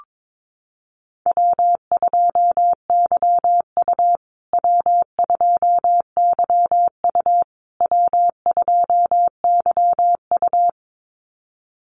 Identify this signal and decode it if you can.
Morse code, heard as a keyed tone, about 22 wpm, 700 Hz. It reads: W2YU W2YU W2YU
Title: USA Call Signs Repeated 3x 22wpm